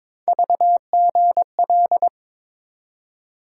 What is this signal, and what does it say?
Morse code, heard as a keyed tone, about 22 wpm, 700 Hz. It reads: VGL
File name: practice / 22wpm / Three Letter-Number Combinations - 22wpm